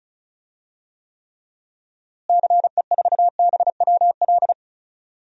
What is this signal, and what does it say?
Morse code, heard as a keyed tone, about 36 wpm, 700 Hz. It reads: CE4BWL
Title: DXCC-100 Call Signs - 6 Characters - Repeated 2x - 35wpm